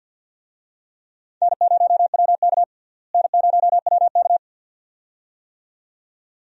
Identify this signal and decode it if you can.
Morse code, heard as a keyed tone, about 52 wpm, 700 Hz. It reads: N0WK N0WK
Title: USA Call Signs - 4 Characters - Repeated 2x - 50wpm